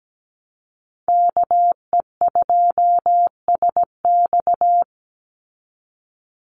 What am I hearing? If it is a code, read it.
Morse code, heard as a keyed tone, about 17 wpm, 700 Hz. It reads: KE2SX
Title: Call Signs Repeated 2x - 17wpm